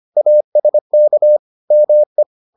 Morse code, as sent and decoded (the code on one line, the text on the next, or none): .- ... -.- / -- .
ASK ME